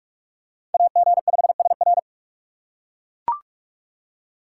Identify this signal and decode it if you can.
Morse code, heard as a keyed tone, about 39 wpm, 700 Hz. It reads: AG5SR